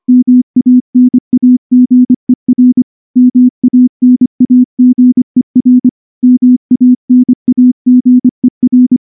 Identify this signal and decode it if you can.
Morse code, heard as a keyed tone, about 24 wpm, 250 Hz. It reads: MANAGER MANAGER MANAGER
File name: morse / morse-training-25WPM